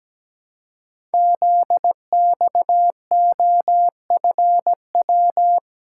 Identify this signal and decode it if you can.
Morse code, heard as a keyed tone, about 17 wpm, 700 Hz. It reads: ZXOFW
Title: Code Groups - 5 Letter:Numbers - 17wpm